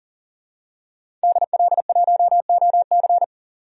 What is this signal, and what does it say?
Morse code, heard as a keyed tone, about 40 wpm, 700 Hz. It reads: DL1OC
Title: DXCC-100 Call Signs - 5 Characters - 40wpm